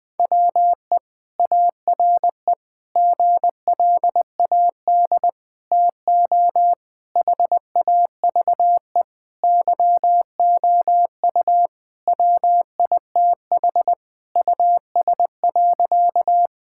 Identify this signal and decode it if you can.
Morse code, heard as a keyed tone, about 20 wpm, 700 Hz. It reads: WE ARE GLAD TO HAVE YOU WITH US.
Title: Sentences with 5-Word Phrases - Spoken Beginning - 20wpm